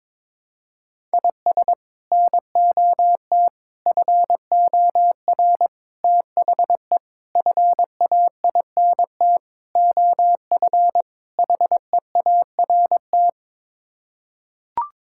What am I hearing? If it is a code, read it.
Morse code, heard as a keyed tone, about 22 wpm, 700 Hz. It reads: IS NOT FOR THE FAINT OF HEART
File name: practice / 22wpm / Sets of 7 Words - 22wpm